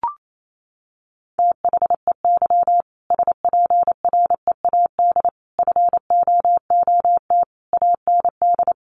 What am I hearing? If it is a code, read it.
Morse code, heard as a keyed tone, about 28 wpm, 700 Hz. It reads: THEY SPREAD FOOT AND
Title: Sentences from Top 1500 Words 28wpm